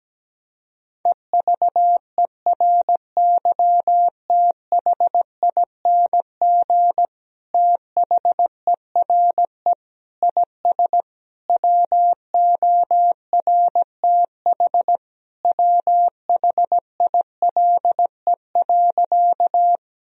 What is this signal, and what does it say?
Morse code, heard as a keyed tone, about 17 wpm, 700 Hz. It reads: EVERYTHING THERE IS WORTH WHILE.
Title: Sentences from Top 1000 Words - Review Part 1 - 17wpm